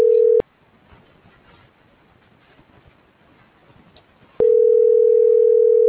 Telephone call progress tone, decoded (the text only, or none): ringback tone